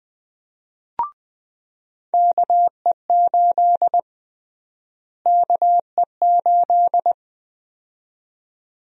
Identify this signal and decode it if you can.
Morse code, heard as a keyed tone, about 20 wpm, 700 Hz. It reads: KE8 KE8
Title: USA Call Sign Prefixes - Repeated 2x - 20wpm